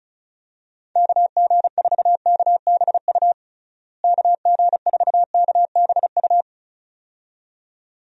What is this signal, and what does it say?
Morse code, heard as a keyed tone, about 36 wpm, 700 Hz. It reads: KG4KBU KG4KBU
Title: USA Call Signs - 6 Characters - Repeated 2x - 35wpm